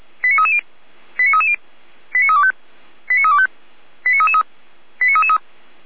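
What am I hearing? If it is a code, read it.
Five-tone selective call (ZVEI), heard as quick runs of five tones, tones of about 80 ms. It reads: 89330 89330 89325 89325 89333
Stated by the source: ZVEI five-tone call received by SDR, from the Feuerwehr Orlamünde audio server.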